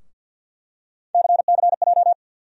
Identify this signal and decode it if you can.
Morse code, heard as a keyed tone, about 47 wpm, 700 Hz. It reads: CCJ
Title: USA Call Sign Postfixes - 3 Letters 50wpm